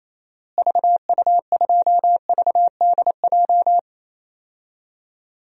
Morse code, heard as a keyed tone, about 27 wpm, 700 Hz. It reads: VU2VDJ
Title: DXCC-100 Call Signs - 6 Characters - Repeated 2x - 28wpm